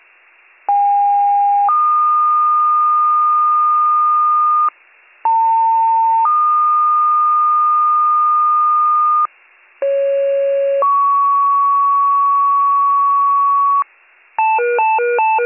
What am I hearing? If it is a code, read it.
Two-tone page, heard as a one-second tone, then a three-second tone, about 810 then 1220 Hz, about 900 then 1220 Hz, about 550 then 1120 Hz. It is heard from 0.7 to 13.8 s.